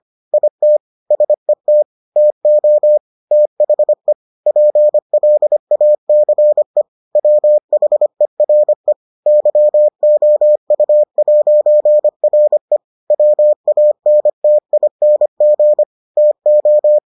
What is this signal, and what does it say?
Morse code, heard as a keyed tone, about 25 wpm, 600 Hz. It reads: IT SET TO THE PLACE WHERE YOU'RE WANTING TO